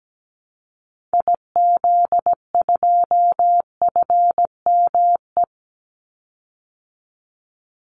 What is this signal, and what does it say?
Morse code, heard as a keyed tone, about 17 wpm, 700 Hz. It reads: IZ2FME
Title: Call Signs Repeated 2x - 17wpm